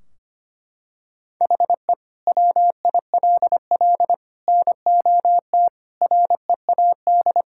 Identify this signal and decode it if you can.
Morse code, heard as a keyed tone, about 25 wpm, 700 Hz. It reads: HE WILL NOT READ
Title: Sentences from Top 1600 Words 25wpm